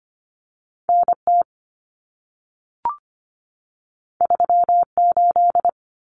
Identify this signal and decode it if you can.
Morse code, heard as a keyed tone, about 25 wpm, 700 Hz. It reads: NT 38
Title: Common Cut Numbers - 2-Digit 25wpm